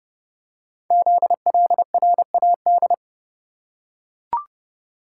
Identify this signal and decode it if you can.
Morse code, heard as a keyed tone, about 30 wpm, 700 Hz. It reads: ZLRAD
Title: Code Groups - 5 Letter:Numbers - 30wpm